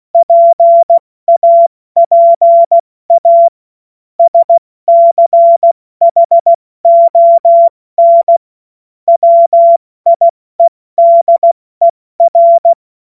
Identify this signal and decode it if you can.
Morse code, heard as a keyed tone, about 16 wpm, spaced as for 14 wpm, 680 Hz. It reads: PAPA SCHON WIEDER
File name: Morsekurs / Witze / Witz 14. März 2023 70 F80